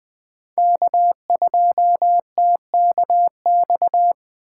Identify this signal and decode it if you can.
Morse code, heard as a keyed tone, about 20 wpm, 700 Hz. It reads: K2TKX
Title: USA Call Signs - 4x Character Spacing - 20wpm